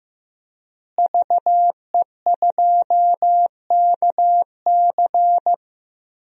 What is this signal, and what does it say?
Morse code, heard as a keyed tone, about 15 wpm, 700 Hz. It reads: VE2KC
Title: DXCC-100 Call Signs - 5 Characters - Repeated 3x - 15wpm